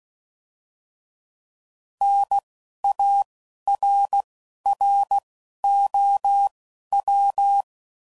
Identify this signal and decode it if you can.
Morse code, heard as a keyed tone, about 16 wpm, spaced as for 11 wpm, 800 Hz. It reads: NARROW
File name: Prefixes / By First Letter / Ns